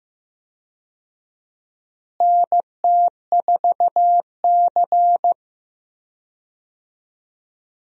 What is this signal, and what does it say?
Morse code, heard as a keyed tone, about 15 wpm, 700 Hz. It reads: NT4C
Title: USA Call Signs - 4 Characters - Repeated 2x - 15wpm